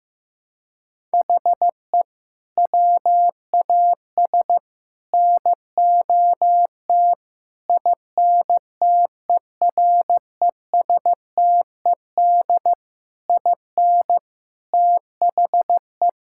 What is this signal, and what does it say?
Morse code, heard as a keyed tone, about 15 wpm, 700 Hz. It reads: HE WAS NOT INTERESTED IN THE
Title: Sentences from Top 1000 Words - Review Part 6 - 15wpm